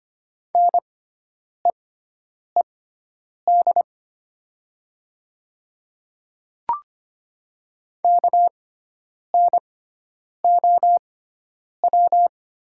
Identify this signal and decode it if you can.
Morse code, heard as a keyed tone, about 25 wpm, spaced as for 9 wpm, 700 Hz. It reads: NEED KNOW
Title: 6x Character Spacing - Top 300 Words - 25wpm